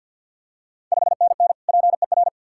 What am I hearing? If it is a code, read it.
Morse code, heard as a keyed tone, about 48 wpm, 690 Hz. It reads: 5NN PER
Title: Ontario QSO Party Exchange 50wpm